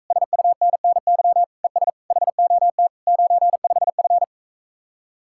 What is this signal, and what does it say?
Morse code, heard as a keyed tone, about 44 wpm, 700 Hz. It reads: SUNNY ES HOT 95F